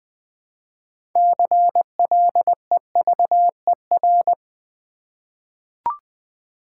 Morse code, heard as a keyed tone, about 20 wpm, 700 Hz. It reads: CLEVER
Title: 6 Letter Words - First 3 Letters Given - 20wpm